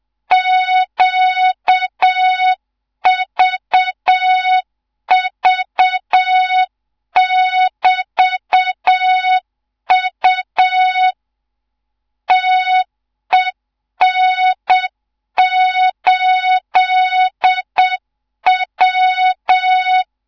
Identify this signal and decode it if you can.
Morse code, heard as a keyed tone, about 7 wpm, 3700 Hz. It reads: QVV<BT>U TEN8W
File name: Morze-HA2MN / hang / 14 280 HA2MN